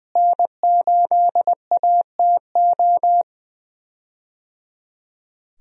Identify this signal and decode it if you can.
Morse code, heard as a keyed tone, about 19 wpm, 700 Hz. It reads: N8ATO
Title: USA Call Signs - 5 Characters - Repeated 3x - 20wpm